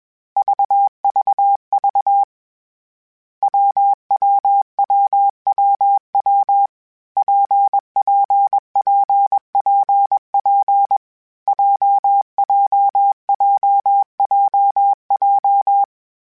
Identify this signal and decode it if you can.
Morse code, heard as a keyed tone, about 21 wpm, 800 Hz. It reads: VVV WWWWW PPPPP JJJJJ
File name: Difficult Combos / Difficult Combos-mp3s / WPJ WPJ1 @20